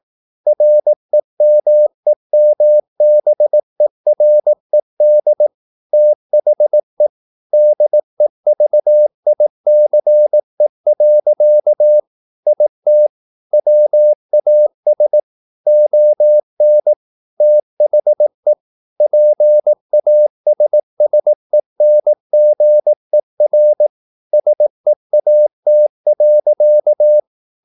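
Morse code, heard as a keyed tone, about 18 wpm, 600 Hz. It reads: REMEMBERED THE DEVICE. IT WAS ON THE PASSENGER SEAT.
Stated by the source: Morse practice at 18 wpm, English text, from CW med Gnister.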